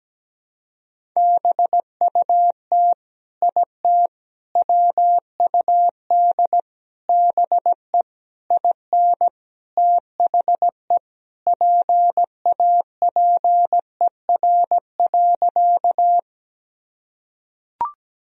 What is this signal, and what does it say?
Morse code, heard as a keyed tone, about 17 wpm, 700 Hz. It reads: BUT IT WUD BE IN THE PAPER.
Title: Sentences from Top 300 Words and Abbreviations 17wpm